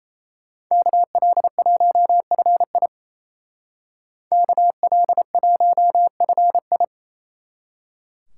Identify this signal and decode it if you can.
Morse code, heard as a keyed tone, about 34 wpm, 700 Hz. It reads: KL1FI KL1FI
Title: Speed Racing - USA 2x2 Call Signs 28wpm